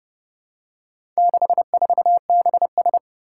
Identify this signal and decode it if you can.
Morse code, heard as a keyed tone, about 30 wpm, 700 Hz. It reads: 64BS
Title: Four Challenging Letter-Number Combinations 30wpm